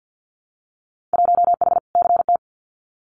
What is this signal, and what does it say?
Morse code, heard as a keyed tone, about 50 wpm, 700 Hz. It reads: 1H CT
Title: Winter Field Day Exchanges 50wpm